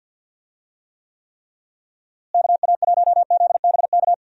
Morse code, heard as a keyed tone, about 46 wpm, 700 Hz. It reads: KA1ZBK
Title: USA Call Signs - 6 Characters - Repeated 3x - 50wpm